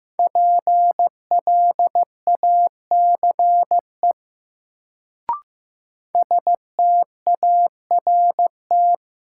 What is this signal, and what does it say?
Morse code, heard as a keyed tone, about 15 wpm, 700 Hz. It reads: PLACE START
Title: Sets of 4 Words - New Word Drill - 15wpm